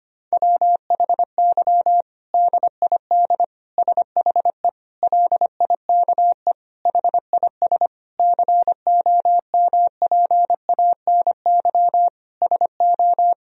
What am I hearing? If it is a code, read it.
Morse code, heard as a keyed tone, about 25 wpm, 700 Hz. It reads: WHY DID SHE LIKE HIS COMPANY SO
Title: Sentences from Top 1000 Words - Review Part 3 - 25wpm